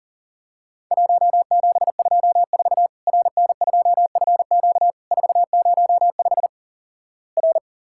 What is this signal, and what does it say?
Morse code, heard as a keyed tone, about 37 wpm, 700 Hz. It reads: 1724 RN2FQ 405 R
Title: Worked All Europe DX Contest - QTC Exchange - 40wpm